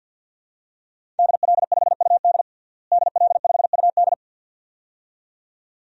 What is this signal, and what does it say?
Morse code, heard as a keyed tone, about 52 wpm, 700 Hz. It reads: DL5UD DL5UD
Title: DXCC-100 Call Signs - 5 Characters - Repeated 2x - 50wpm